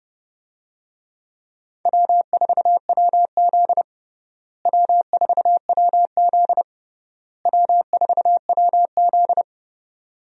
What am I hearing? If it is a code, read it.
Morse code, heard as a keyed tone, about 30 wpm, 700 Hz. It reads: W4WZ W4WZ W4WZ
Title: USA Call Signs - 4 Characters - Repeated 3x - 30wpm